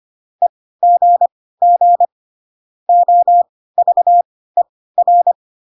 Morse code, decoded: EGG OVER

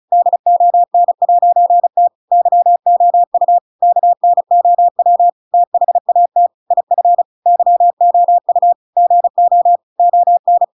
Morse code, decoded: DON'T YOU KNOW THAT IF YOU GO ON

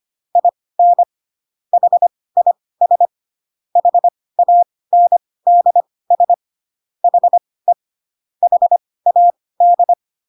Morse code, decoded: IN HIS HANDS HE HAD

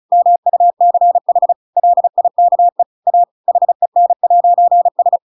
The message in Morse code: MUCH LIKE A HEN'S

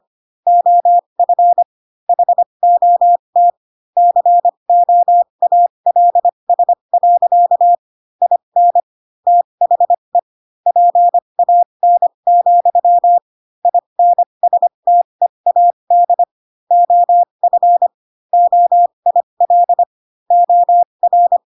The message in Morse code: OF HOT COALS. IN THE PAN, INSTEAD OF OIL OR

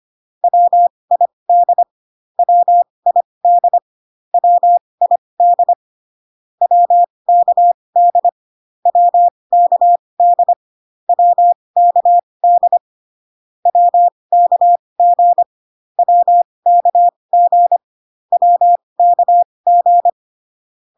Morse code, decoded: WID WID WID WKD WKD WKD WKG WKG WKG

